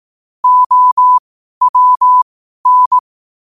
Morse code, heard as a keyed tone, about 18 wpm, spaced as for 12 wpm, 1000 Hz. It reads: OWN
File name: qotdfiles / 12wpm